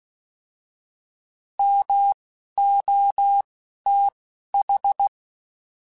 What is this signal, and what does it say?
Morse code, heard as a keyed tone, about 16 wpm, 800 Hz. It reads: M O T H